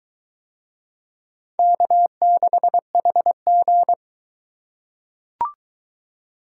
Morse code, heard as a keyed tone, about 21 wpm, 700 Hz. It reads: K6HG